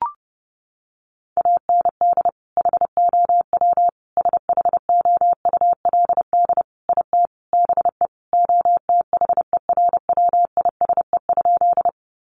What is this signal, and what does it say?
Morse code, heard as a keyed tone, about 30 wpm, 700 Hz. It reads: AND HOW SHOULD IT BE OTHERWISE?
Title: Sentences from Top 1200 Words 30wpm